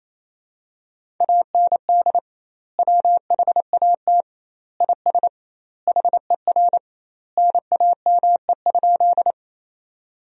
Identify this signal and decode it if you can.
Morse code, heard as a keyed tone, about 27 wpm, 700 Hz. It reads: AND WHAT IS HER NAME?
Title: 2x Word-Spacing - Sentences from Top 200 Words - 28wpm